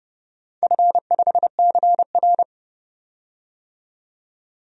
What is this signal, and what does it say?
Morse code, heard as a keyed tone, about 30 wpm, 700 Hz. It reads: F5CR
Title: DXCC-100 Call Signs - 4 Characters - 30wpm